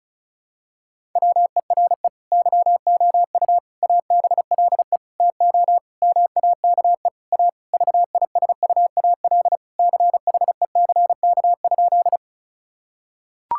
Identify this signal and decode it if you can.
Morse code, heard as a keyed tone, about 36 wpm, 700 Hz. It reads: WERE YOU ABLE TO MAKE A VISUAL CHECK?
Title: Sentences from Top 1000 Words - Review Part 6 - 35wpm